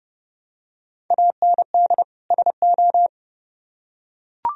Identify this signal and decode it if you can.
Morse code, heard as a keyed tone, about 30 wpm, 700 Hz. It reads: AND SO